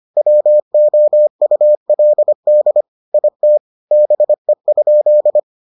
Morse code, heard as a keyed tone, about 25 wpm, 600 Hz. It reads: WOULD IT BE?